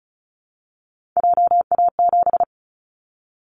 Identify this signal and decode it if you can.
Morse code, heard as a keyed tone, about 36 wpm, 690 Hz. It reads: JA7
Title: Callsign Prefixes for Top 50 DXCC - 35wpm